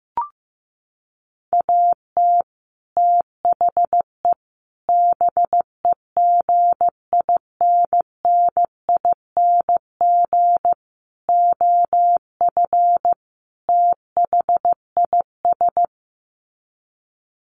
Morse code, heard as a keyed tone, about 15 wpm, 700 Hz. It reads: AT THE BEGINNING OF THIS
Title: Sets of 5 Words Encore 15wpm